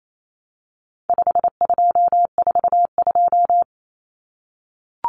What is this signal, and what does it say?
Morse code, heard as a keyed tone, about 28 wpm, 700 Hz. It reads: 5242
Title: Numbers - 4-Digit 28wpm